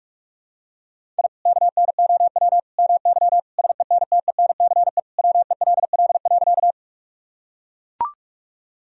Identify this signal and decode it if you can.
Morse code, heard as a keyed tone, about 41 wpm, 700 Hz. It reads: I KNOW MY SENTENCE WELL.
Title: Sentences from Top 1000 Words - Review Part 1 - 45wpm